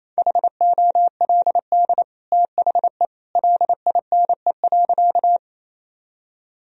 Morse code, heard as a keyed tone, about 28 wpm, 700 Hz. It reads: HOLD THE LINE.
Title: Sentences from Top 1000 Words - Review Part 2 - 28wpm